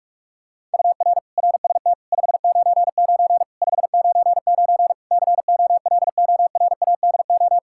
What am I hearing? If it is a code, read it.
Morse code, heard as a keyed tone, about 46 wpm, 700 Hz. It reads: UR RST 599 599 COLORADO
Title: K1USN SST QSO Practice 45wpm